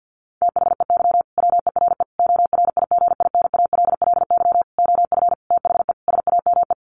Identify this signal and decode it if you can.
Morse code, heard as a keyed tone, about 50 wpm, 700 Hz. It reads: THEY WERE ORIGINALLY OF THE SAME
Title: Sentences from Top 1500 Words 50wpm